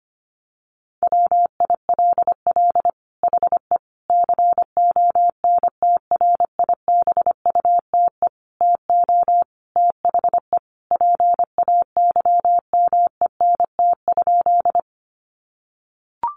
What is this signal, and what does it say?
Morse code, heard as a keyed tone, about 25 wpm, 700 Hz. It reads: WILL HE CONTRIBUTE TO THE PAYMENT?
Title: Sentences from Top 1400 Words 25wpm